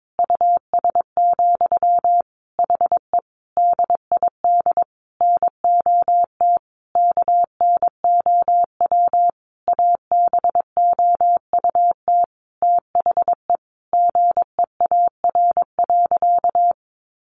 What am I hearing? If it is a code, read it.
Morse code, heard as a keyed tone, about 22 wpm, 700 Hz. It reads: US, HE DID NOT KNOW ABOUT THE GEAR.
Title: Sentences from Top 1400 Words 22wpm